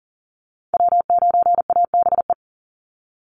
Morse code, heard as a keyed tone, about 40 wpm, 700 Hz. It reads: W9ABE
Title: USA Call Signs 40wpm